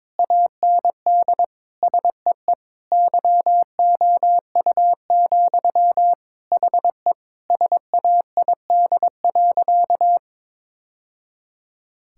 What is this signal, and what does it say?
Morse code, heard as a keyed tone, about 21 wpm, 700 Hz. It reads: AND SEE YOU, HE SAID.
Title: Sentences from Top 1000 Words - Review Part 6 - 22wpm